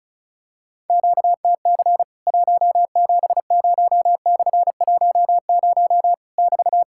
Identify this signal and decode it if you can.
Morse code, heard as a keyed tone, about 36 wpm, 700 Hz. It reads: QTC 170/10 <BT>
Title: Worked All Europe DX Contest - QTC Exchange - 35wpm